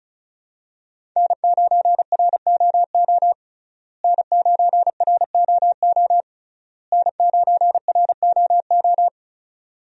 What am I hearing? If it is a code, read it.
Morse code, heard as a keyed tone, about 36 wpm, 700 Hz. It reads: N9ROO N9ROO N9ROO
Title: USA Call Signs - 5 Characters - Repeated 3x - 35wpm